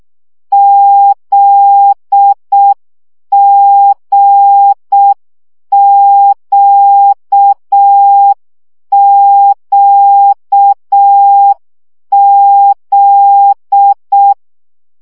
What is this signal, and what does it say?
Morse code, heard as a keyed tone, about 6 wpm, 800 Hz. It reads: ZGQQZ
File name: CW gyakorlas g z q 5 baud